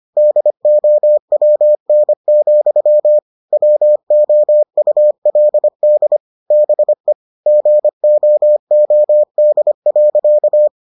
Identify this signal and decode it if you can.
Morse code, heard as a keyed tone, about 25 wpm, 600 Hz. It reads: DOWN, WOULD BE GOOD.